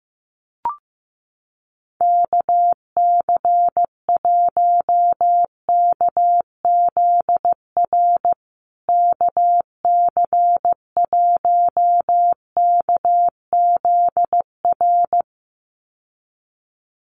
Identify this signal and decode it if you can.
Morse code, heard as a keyed tone, about 15 wpm, 700 Hz. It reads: KC1KZR KC1KZR
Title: USA Call Signs Repeated 2x 15wpm